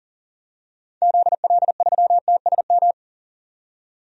Morse code, heard as a keyed tone, about 40 wpm, 700 Hz. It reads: ZL3TSM
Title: DXCC-100 Call Signs - 6 Characters - 40wpm